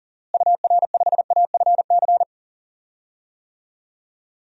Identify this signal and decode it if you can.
Morse code, heard as a keyed tone, about 40 wpm, 700 Hz. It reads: UR5AFC
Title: DXCC-100 Call Signs - 6 Characters - Repeated 2x - 40wpm